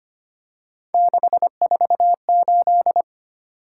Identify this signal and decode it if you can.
Morse code, heard as a keyed tone, about 25 wpm, 700 Hz. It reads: 648
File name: Three Letter-Number Combinations - 25wpm